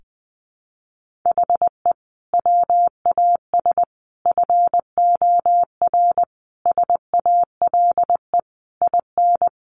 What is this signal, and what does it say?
Morse code, heard as a keyed tone, about 20 wpm, 700 Hz. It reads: HE WAS FOR SALE IN
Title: Sentences from Top 1400 Words 20wpm